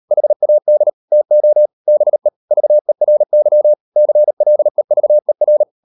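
Morse code, heard as a keyed tone, about 38 wpm, 600 Hz. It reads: HAD TO BE VERY CLEVER
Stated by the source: Morse code practice at 38 WPM, sending Ken Follett's novel Never in English, from CW med Gnister.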